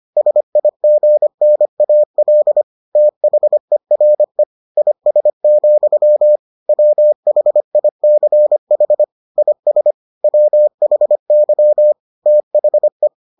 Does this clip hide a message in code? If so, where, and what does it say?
Yes: Morse code from 0.2 to 13.1 s, heard as a keyed tone, about 25 wpm, 600 Hz. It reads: SIGNAL THERE IS, WHICH IS WHY THE